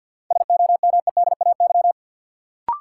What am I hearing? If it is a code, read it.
Morse code, heard as a keyed tone, about 50 wpm, 700 Hz. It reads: SOMEDAY
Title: Sentences from Top 1600 Words - New Word Drill Alt - 50wpm